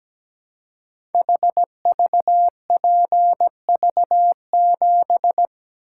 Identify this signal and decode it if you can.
Morse code, heard as a keyed tone, about 17 wpm, 700 Hz. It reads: HVPV7
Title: Code Groups - 5 Letter:Numbers - 17wpm